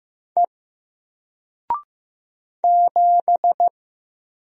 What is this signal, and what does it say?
Morse code, heard as a keyed tone, about 15 wpm, 700 Hz. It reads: E 7
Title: Single Letter-Number Alt 15wpm